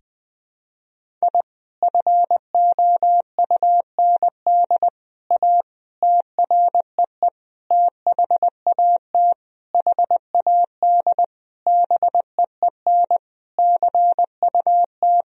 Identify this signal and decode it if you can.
Morse code, heard as a keyed tone, about 20 wpm, 700 Hz. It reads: I FOUND A TREE THAT HAD BEEN CUT
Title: Sentences from Top 1000 Words - Review Part 1 - 20wpm